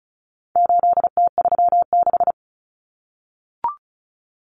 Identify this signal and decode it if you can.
Morse code, heard as a keyed tone, about 34 wpm, 700 Hz. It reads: 8T36
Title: Common Cut Numbers - 4-Digit 35wpm